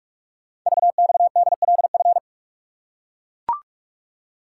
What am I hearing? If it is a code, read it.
Morse code, heard as a keyed tone, about 44 wpm, 700 Hz. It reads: VXDLF
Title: Code Groups - 5 Letters - 45wpm